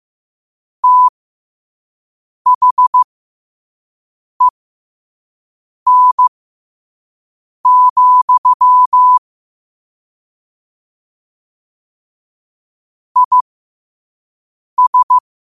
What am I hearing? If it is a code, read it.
Morse code, heard as a keyed tone, about 15 wpm, spaced as for 5 wpm, 1000 Hz. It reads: THEN, IS
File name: qotdfiles / 5wpm